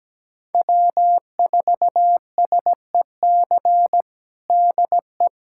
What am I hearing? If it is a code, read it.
Morse code, heard as a keyed tone, about 17 wpm, 700 Hz. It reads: W4SEC DE
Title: USA Callsign DE Callsign - 5 Characters - 17wpm